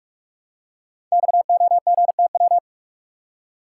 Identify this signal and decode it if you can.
Morse code, heard as a keyed tone, about 44 wpm, 690 Hz. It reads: XOGTW
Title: Code Groups - 5 Letter:Numbers - 45wpm